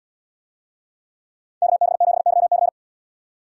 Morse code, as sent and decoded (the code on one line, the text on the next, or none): -.- -... --... -.-- -.-.
KB7YC